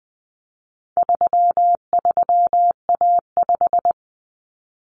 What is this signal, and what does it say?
Morse code, heard as a keyed tone, about 20 wpm, 700 Hz. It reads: 33A5
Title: Common Cut Numbers - 4-Digit 20wpm